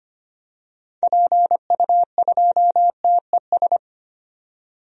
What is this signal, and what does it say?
Morse code, heard as a keyed tone, about 25 wpm, 700 Hz. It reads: PU2TES
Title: DXCC-100 Call Signs - 6 Characters - 25wpm